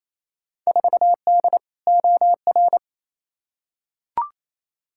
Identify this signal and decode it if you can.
Morse code, heard as a keyed tone, about 28 wpm, 700 Hz. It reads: 4D OR
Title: Field Day Exchanges Encore - 28wpm